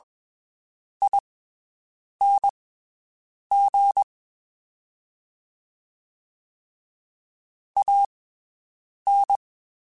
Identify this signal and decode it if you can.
Morse code, heard as a keyed tone, about 21 wpm, spaced as for 7 wpm, 800 Hz. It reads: ING AN
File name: (spaced) / QSOs Spaced-mp3s / WD0FBT 1 @20 sp